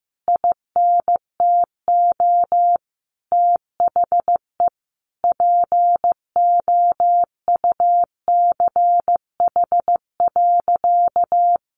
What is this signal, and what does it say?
Morse code, heard as a keyed tone, about 15 wpm, 700 Hz. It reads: INTO THE POUCH.
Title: Easy Phrases 15wpm